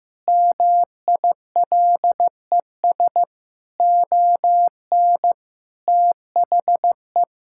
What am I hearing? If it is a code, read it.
Morse code, heard as a keyed tone, about 16 wpm, 700 Hz. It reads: MILES ON THE
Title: Pangram Challenge v2 - 15wpm